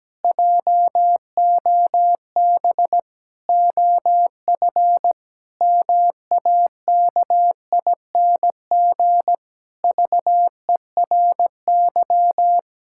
Morse code, heard as a keyed tone, about 17 wpm, 700 Hz. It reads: JOB OF MAKING VERY
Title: Pangram Challenge v2 - 17wpm